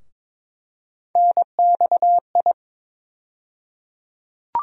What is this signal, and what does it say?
Morse code, heard as a keyed tone, about 22 wpm, 700 Hz. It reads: NXI
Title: USA Call Sign Postfixes - 3 Letters 22wpm